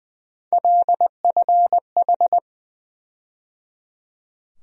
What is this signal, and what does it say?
Morse code, heard as a keyed tone, about 20 wpm, 700 Hz. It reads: LFH
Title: USA Call Sign Postfixes - 3 Letters 20wpm